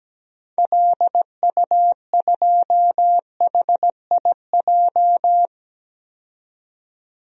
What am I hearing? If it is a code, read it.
Morse code, heard as a keyed tone, about 17 wpm, 700 Hz. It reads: LU2HIJ
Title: DXCC-100 Call Signs - 6 Characters - 17wpm